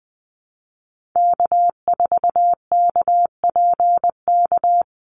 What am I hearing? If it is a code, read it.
Morse code, heard as a keyed tone, about 20 wpm, 700 Hz. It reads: K4KPK
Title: NA SOTA Activator Callsigns 20wpm